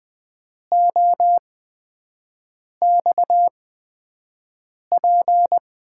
Morse code, heard as a keyed tone, about 20 wpm, spaced as for 5 wpm, 700 Hz. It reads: OXP
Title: USA Call Signs - 6 Characters - 8x Character Spacing - 20wpm